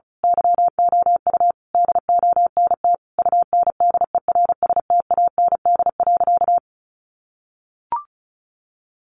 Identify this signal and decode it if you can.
Morse code, heard as a keyed tone, about 34 wpm, 700 Hz. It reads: YOU DONT UNDERSTAND.
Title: Sentences from Top 600 Words 35wpm